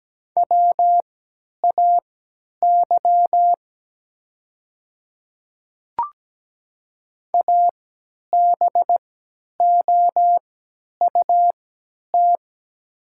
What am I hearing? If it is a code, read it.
Morse code, heard as a keyed tone, about 17 wpm, spaced as for 10 wpm, 700 Hz. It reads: WAY ABOUT
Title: 3x Character Spacing - Top 200 Words - 17wpm